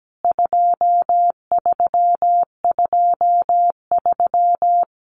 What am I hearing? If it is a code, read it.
Morse code, heard as a keyed tone, about 17 wpm, 700 Hz. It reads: 2323
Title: CWT Exchange 17wpm